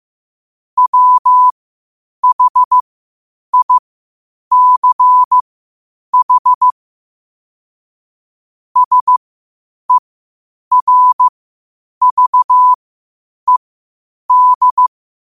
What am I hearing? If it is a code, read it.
Morse code, heard as a keyed tone, about 15 wpm, spaced as for 8 wpm, 1000 Hz. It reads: WHICH SERVED
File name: qotdfiles / 8wpm